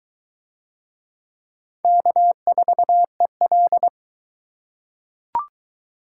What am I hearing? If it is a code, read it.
Morse code, heard as a keyed tone, about 20 wpm, 700 Hz. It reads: K4EL